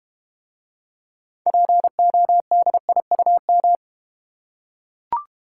Morse code, heard as a keyed tone, about 31 wpm, 700 Hz. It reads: PODIUM
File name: Easy Words - Speed-Racing with Hints - 25wpm